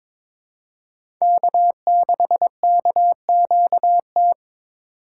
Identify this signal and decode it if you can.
Morse code, heard as a keyed tone, about 22 wpm, 700 Hz. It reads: K6KQT
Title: USA Call Signs - 8x Character Spacing - 22wpm